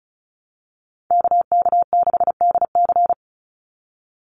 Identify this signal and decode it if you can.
Morse code, heard as a keyed tone, about 34 wpm, 700 Hz. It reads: KK6DC
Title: USA Call Signs 35wpm